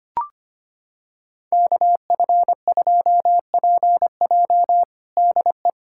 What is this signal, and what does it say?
Morse code, heard as a keyed tone, about 25 wpm, 700 Hz. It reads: KF2PJ DE
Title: USA Callsign DE Callsign - 5 Characters - 25wpm